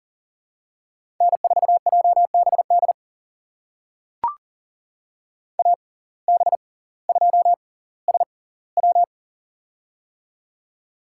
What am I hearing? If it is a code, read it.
Morse code, heard as a keyed tone, about 38 wpm, spaced as for 20 wpm, 700 Hz. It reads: N4JBD AB2SW
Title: USA Call Signs - 5 Characters - 6x Character Spacing - 40wpm